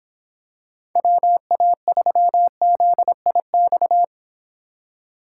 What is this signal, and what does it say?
Morse code, heard as a keyed tone, about 25 wpm, 700 Hz. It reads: WA3ZIX